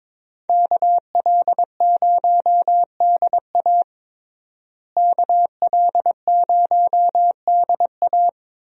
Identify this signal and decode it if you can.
Morse code, heard as a keyed tone, about 22 wpm, 700 Hz. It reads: KL0DA KL0DA
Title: USA Call Signs - 5 Characters - Repeated 2x - 22wpm